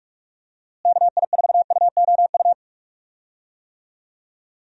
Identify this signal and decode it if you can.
Morse code, heard as a keyed tone, about 41 wpm, 700 Hz. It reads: KI4UOU
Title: USA Call Signs - 6 Characters - Repeated 3x - 45wpm